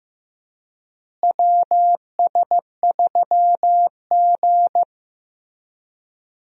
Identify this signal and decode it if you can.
Morse code, heard as a keyed tone, about 15 wpm, 700 Hz. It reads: WS3G
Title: USA Call Signs - 4 Characters - Repeated 3x - 15wpm